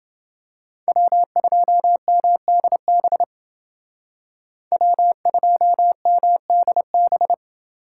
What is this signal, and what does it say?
Morse code, heard as a keyed tone, about 29 wpm, 700 Hz. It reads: W2MDB W2MDB